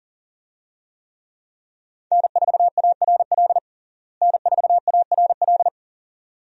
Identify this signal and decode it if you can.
Morse code, heard as a keyed tone, about 40 wpm, 700 Hz. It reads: N4ARL N4ARL
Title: USA Call Signs - 5 Characters - Repeated 2x - 40wpm